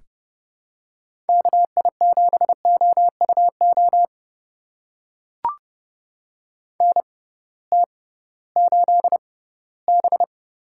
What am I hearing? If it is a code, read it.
Morse code, heard as a keyed tone, about 30 wpm, spaced as for 18 wpm, 700 Hz. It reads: KI7OUO NT8B
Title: USA Call Signs - 6x Character Spacing - 30wpm